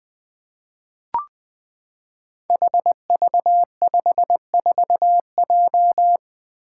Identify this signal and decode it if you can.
Morse code, heard as a keyed tone, about 20 wpm, 700 Hz. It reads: HV54J
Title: Five Challenging Letter-Number Combinations 20wpm